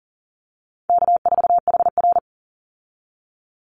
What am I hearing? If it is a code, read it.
Morse code, heard as a keyed tone, about 40 wpm, 700 Hz. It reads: K4HR
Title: Call Signs Repeated 2x - 40wpm